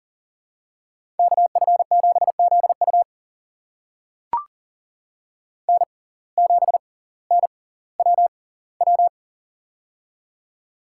Morse code, heard as a keyed tone, about 39 wpm, spaced as for 20 wpm, 700 Hz. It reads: KF7ZU N7NWW
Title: USA Call Signs - 5 Characters - 6x Character Spacing - 40wpm